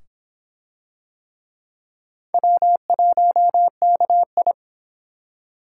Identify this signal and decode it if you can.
Morse code, heard as a keyed tone, about 25 wpm, 700 Hz. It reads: W1KI